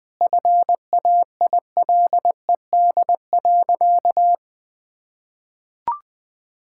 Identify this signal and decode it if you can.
Morse code, heard as a keyed tone, about 20 wpm, 700 Hz. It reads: FAILED.
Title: Sentences with 9-Word Phrases - 20wpm